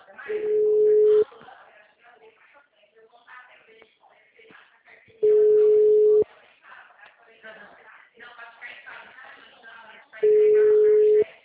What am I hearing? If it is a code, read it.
Ringback tone (Europe, 425 Hz), with speech.